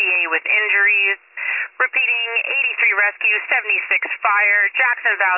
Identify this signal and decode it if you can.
Emergency Alert System (SAME) tones, heard as harsh buzzing bursts, with speech. It is heard from 1.1 to 4.8 s.